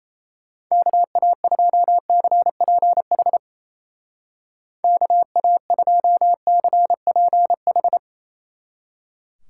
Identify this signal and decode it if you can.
Morse code, heard as a keyed tone, about 36 wpm, 700 Hz. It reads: KA2CPH KA2CPH